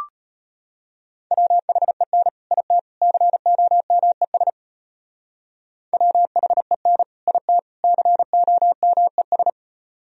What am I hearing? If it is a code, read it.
Morse code, heard as a keyed tone, about 31 wpm, 700 Hz. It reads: WHEN IT COMES WHEN IT COMES
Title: Speed-Racing - Sets of 3 Words 25wpm